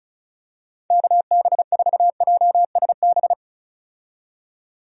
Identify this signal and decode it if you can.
Morse code, heard as a keyed tone, about 34 wpm, 700 Hz. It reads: KB4JSB
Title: USA Call Signs - 6 Characters - 35wpm